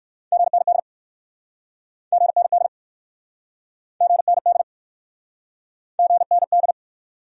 Morse code, heard as a keyed tone, about 60 wpm, 700 Hz. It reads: GND GND GND GND